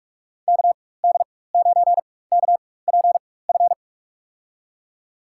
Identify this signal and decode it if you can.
Morse code, heard as a keyed tone, about 46 wpm, spaced as for 21 wpm, 700 Hz. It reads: KD9KPF
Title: USA Call Signs - 6 Characters - 4x Character Spacing - 45wpm